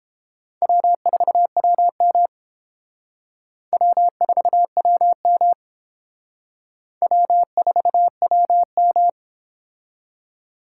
Morse code, heard as a keyed tone, about 28 wpm, 700 Hz. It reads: W4WM W4WM W4WM